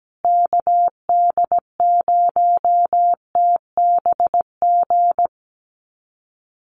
Morse code, heard as a keyed tone, about 17 wpm, 700 Hz. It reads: KD0TBG
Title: USA Call Signs Repeated 3x 17wpm